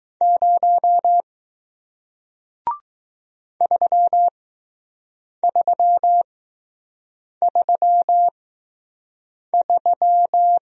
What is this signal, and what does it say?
Morse code, heard as a keyed tone, about 20 wpm, spaced as for 6 wpm, 700 Hz. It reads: 0 3333